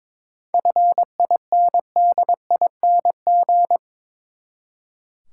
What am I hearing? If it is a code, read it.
Morse code, heard as a keyed tone, about 22 wpm, 700 Hz. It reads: FINDING
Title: Seven Letter Words - 22wpm